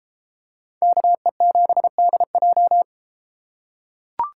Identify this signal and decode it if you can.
Morse code, heard as a keyed tone, about 31 wpm, 700 Hz. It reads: KE7DJ